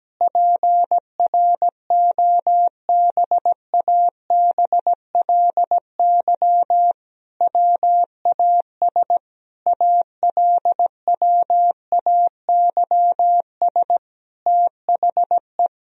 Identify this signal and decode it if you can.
Morse code, heard as a keyed tone, about 17 wpm, 700 Hz. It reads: PROBABLY WAS ALWAYS THE
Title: Sentences from Top 1000 Words - Review Part 4 - 17wpm